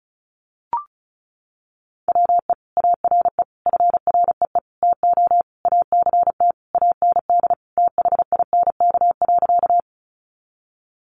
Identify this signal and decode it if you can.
Morse code, heard as a keyed tone, about 34 wpm, 700 Hz. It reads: WE ARE FREE TO ACT AND THINK.
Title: Sentences from Top 500 Words 35wpm